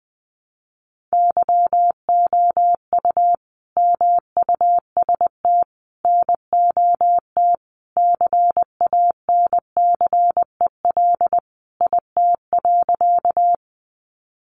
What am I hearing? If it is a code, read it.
Morse code, heard as a keyed tone, about 20 wpm, 700 Hz. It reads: YOU MUST NOT CANCEL IT.
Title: Sentences from Top 1200 Words 20wpm